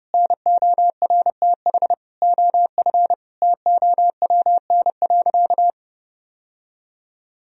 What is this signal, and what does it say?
Morse code, heard as a keyed tone, about 29 wpm, 700 Hz. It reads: NORTH OF TOWN.
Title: Sentences from Top 1000 Words - Review Part 4 - 30wpm